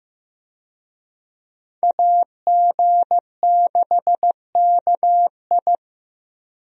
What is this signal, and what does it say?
Morse code, heard as a keyed tone, about 15 wpm, 700 Hz. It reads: AG6KI
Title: USA Call Signs - 5 Characters - Repeated 3x - 15wpm